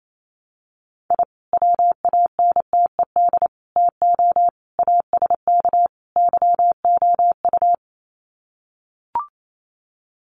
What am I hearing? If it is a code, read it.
Morse code, heard as a keyed tone, about 28 wpm, 700 Hz. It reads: I WANTED TO ASK YOU
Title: Sets of 5 Words Encore2 28wpm